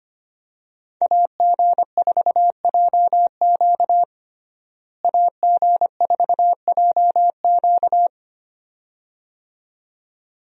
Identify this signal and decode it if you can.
Morse code, heard as a keyed tone, about 25 wpm, 700 Hz. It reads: AG4JQ AG4JQ
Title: USA Call Signs - 5 Characters - Repeated 2x - 25wpm